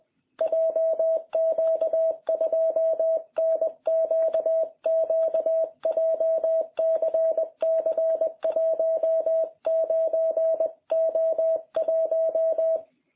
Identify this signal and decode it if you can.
Morse code, heard as a keyed tone, about 21 wpm, 660 Hz. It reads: JQ2NQQ JCC 19O1